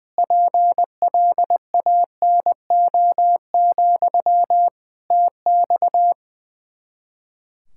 Morse code, heard as a keyed tone, about 20 wpm, 700 Hz. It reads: PLANO, TX
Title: K1USN SST QSO Practice 20wpm